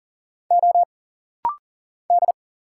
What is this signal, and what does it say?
Morse code, heard as a keyed tone, about 39 wpm, 700 Hz. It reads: O D
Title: Single Letters - Rapid-Fire 40wpm